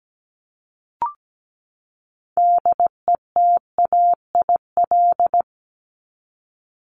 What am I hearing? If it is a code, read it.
Morse code, heard as a keyed tone, about 17 wpm, 700 Hz. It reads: DETAIL
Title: Top 1251-1500 Words 17wpm